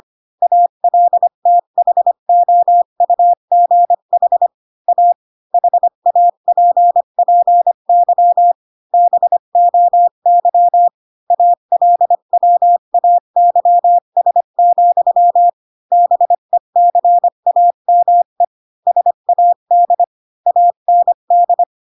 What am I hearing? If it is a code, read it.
Morse code, heard as a keyed tone, about 25 wpm, 700 Hz. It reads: ALTHOUGH A HAPPY BOY ALWAYS, BECAME SAD AND
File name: Pinocchio / Pinocchio ch8 20 25 ik1hge